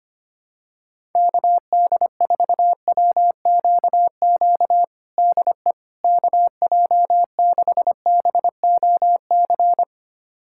Morse code, heard as a keyed tone, about 25 wpm, 700 Hz. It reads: KD4WQQ DE KJ6BOC
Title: USA Callsign DE Callsign - 6 Characters - 25wpm